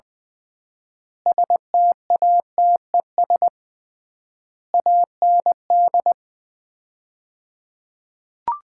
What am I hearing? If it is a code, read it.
Morse code, heard as a keyed tone, about 21 wpm, 700 Hz. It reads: STATES AND
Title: 3x Word-Spacing - Sets of 5 Words - 20wpm